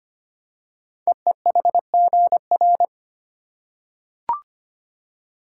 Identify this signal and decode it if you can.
Morse code, heard as a keyed tone, about 25 wpm, 700 Hz. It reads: EEHGR
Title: Code Groups - 5 Letters - 25wpm